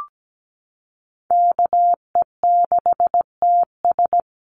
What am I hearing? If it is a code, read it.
Morse code, heard as a keyed tone, about 17 wpm, 700 Hz. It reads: KE6TS
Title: USA Call Signs Repeated 3x 17wpm